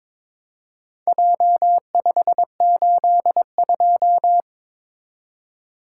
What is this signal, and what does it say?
Morse code, heard as a keyed tone, about 22 wpm, 700 Hz. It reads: J582
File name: practice / 22wpm / Four Letter-Number Combinations - 22wpm